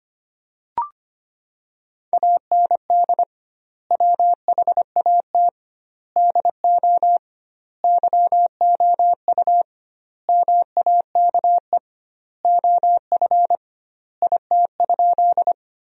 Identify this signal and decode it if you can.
Morse code, heard as a keyed tone, about 25 wpm, 700 Hz. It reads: AND WHAT DO YOU MAKE OF IT?
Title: 2x Word-Spacing - Sentences from Top 100 Words - 25wpm